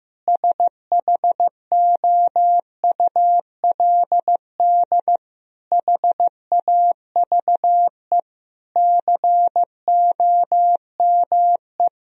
Morse code, heard as a keyed tone, about 15 wpm, 700 Hz. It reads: SHOULD HAVE COME
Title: Sentences from Top 1000 Words - Review Part 4 - 15wpm